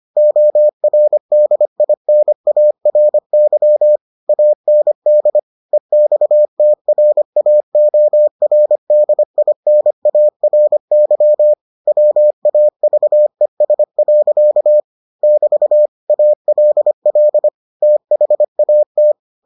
Morse code, heard as a keyed tone, about 25 wpm, 600 Hz. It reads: ORDINARY AND EXTRAORDINARY WAVES. <BT> ALL THAT